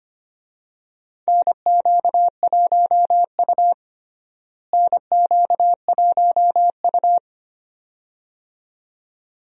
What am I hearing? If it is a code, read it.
Morse code, heard as a keyed tone, about 25 wpm, 700 Hz. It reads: NQ1U NQ1U
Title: USA Call Signs - 4 Characters - Repeated 2x - 25wpm